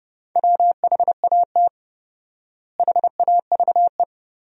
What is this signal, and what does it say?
Morse code, heard as a keyed tone, about 30 wpm, 700 Hz. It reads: WHAT HAVE
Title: 4x Word-Spacing - Sentences from Top 100 Words - 30wpm